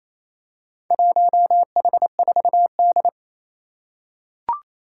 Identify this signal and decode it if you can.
Morse code, heard as a keyed tone, about 28 wpm, 700 Hz. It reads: 1H4D
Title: Four Challenging Letter-Number Combinations 28wpm